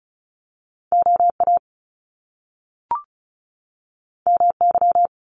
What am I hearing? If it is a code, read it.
Morse code, heard as a keyed tone, about 36 wpm, 700 Hz. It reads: OA MY
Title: Top Two Letter Combinations in Words 35wpm